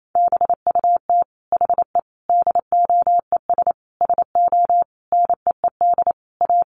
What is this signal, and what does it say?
Morse code, heard as a keyed tone, about 28 wpm, 700 Hz. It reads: BUT HE DOES SO NEED A
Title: Sentences from Top 200 Words 28wpm